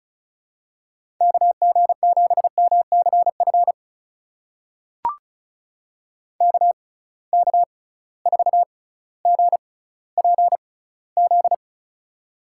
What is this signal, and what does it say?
Morse code, heard as a keyed tone, about 36 wpm, spaced as for 18 wpm, 700 Hz. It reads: KG7MCF KK4GPZ
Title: USA Call Signs - 6 Characters - 6x Character Spacing - 35wpm